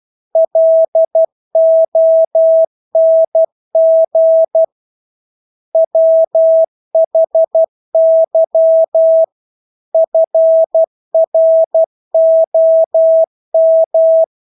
Morse code, heard as a keyed tone, about 12 wpm, 650 Hz. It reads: LONG WHY FROM
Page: Engelske ord 12 wpm